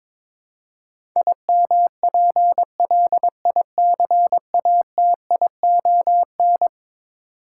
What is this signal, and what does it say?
Morse code, heard as a keyed tone, about 22 wpm, 700 Hz. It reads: IMPLICATION
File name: practice / 22wpm / Eleven Letter Words - 22wpm